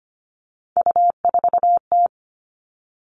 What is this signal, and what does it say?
Morse code, heard as a keyed tone, about 25 wpm, 700 Hz. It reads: U4T
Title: All Cut Numbers - 3-Digit 25wpm